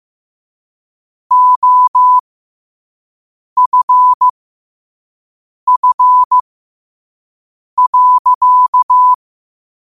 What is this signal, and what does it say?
Morse code, heard as a keyed tone, about 15 wpm, spaced as for 5 wpm, 1000 Hz. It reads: OFF.
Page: New quotes every day in morse code at 5 Words per minute.